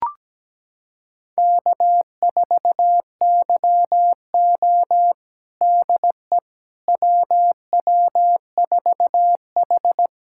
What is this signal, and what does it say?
Morse code, heard as a keyed tone, about 17 wpm, 700 Hz. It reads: K4YO DE WW4H
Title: USA Callsign DE Callsign - 4 Characters - 17wpm